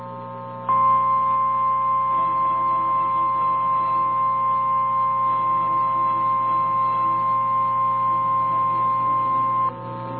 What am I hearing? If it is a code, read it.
Emergency Alert System (SAME) tones (the 1050 Hz attention tone), with noise. It is heard from 0.6 to 9.8 s.
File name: eas / KNCYKBIE / KNCYKBIE